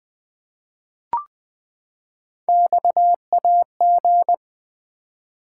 Morse code, heard as a keyed tone, about 20 wpm, 700 Hz. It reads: XAG
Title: Three Letter-Number Combinations - 20wpm